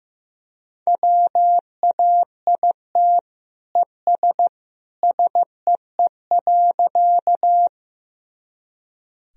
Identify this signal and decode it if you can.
Morse code, heard as a keyed tone, about 15 wpm, 700 Hz. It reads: WAIT ES SEE.
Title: Sentences from Top 500 Words and Abbreviations 15wpm